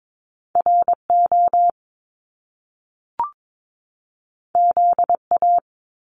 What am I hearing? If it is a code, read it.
Morse code, heard as a keyed tone, about 22 wpm, 700 Hz. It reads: RO ZA
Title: Top Two Letter Combinations in Words 22wpm